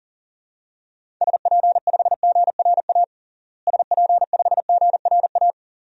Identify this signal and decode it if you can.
Morse code, heard as a keyed tone, about 37 wpm, 700 Hz. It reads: SP5GRA SP5GRA
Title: DXCC-100 Call Signs - 6 Characters - Repeated 2x - 40wpm